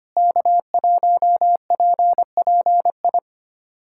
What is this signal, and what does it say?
Morse code, heard as a keyed tone, about 25 wpm, 700 Hz. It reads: K1PPI
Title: USA Callsign DE Callsign - 5 Characters - 25wpm